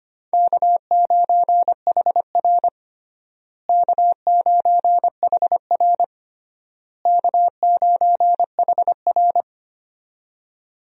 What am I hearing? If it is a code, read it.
Morse code, heard as a keyed tone, about 25 wpm, 700 Hz. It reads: K9HR K9HR K9HR
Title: USA Call Signs - 4 Characters - Repeated 3x - 25wpm